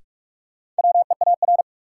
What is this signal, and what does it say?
Morse code, heard as a keyed tone, about 41 wpm, 700 Hz. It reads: WEAR
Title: Sentences from Top 1500 Words - New Word Drill Alt - 45wpm